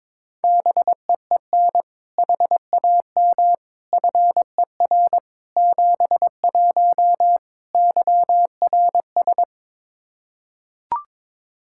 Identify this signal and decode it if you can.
Morse code, heard as a keyed tone, about 22 wpm, 700 Hz. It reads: BEEN HAM FER 71 YRS
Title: Practice QSO - Exchange 3 - More Info - 22wpm